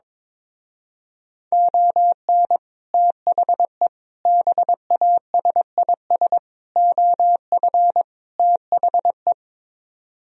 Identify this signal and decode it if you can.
Morse code, heard as a keyed tone, about 23 wpm, 700 Hz. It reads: ON THE BASIS OF THE
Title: Speed-Racing - Sets of 5 Words 22wpm